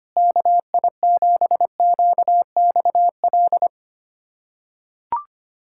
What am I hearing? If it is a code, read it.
Morse code, heard as a keyed tone, about 25 wpm, 700 Hz. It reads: KI7QXL
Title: USA Callsign DE Callsign - 6 Characters - 25wpm